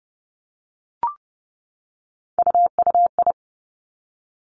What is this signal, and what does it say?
Morse code, heard as a keyed tone, about 30 wpm, 700 Hz. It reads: UUI
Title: Three Challenging Letter-Number Combinations 30wpm